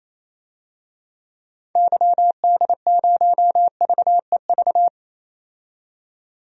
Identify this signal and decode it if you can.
Morse code, heard as a keyed tone, about 28 wpm, 700 Hz. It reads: YD0VEV
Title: DXCC-100 Call Signs - 6 Characters - 28wpm